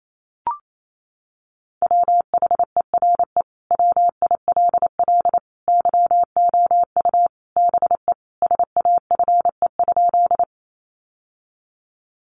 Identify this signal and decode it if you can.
Morse code, heard as a keyed tone, about 28 wpm, 700 Hz. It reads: WHERE WILL YOU BE SAFE?
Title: Sentences from Top 900 Words 28wpm